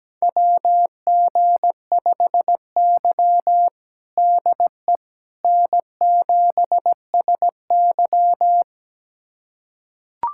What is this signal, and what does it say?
Morse code, heard as a keyed tone, about 17 wpm, 700 Hz. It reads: WG5Y DE N7SY
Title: USA Callsign DE Callsign - 4 Characters - 17wpm